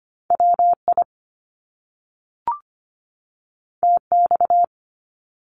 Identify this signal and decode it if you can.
Morse code, heard as a keyed tone, about 25 wpm, 700 Hz. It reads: WI TX
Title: US States 25wpm